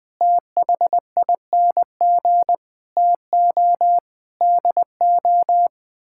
Morse code, heard as a keyed tone, about 20 wpm, 700 Hz. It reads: THING TO DO
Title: Sets of 8 Words - 20wpm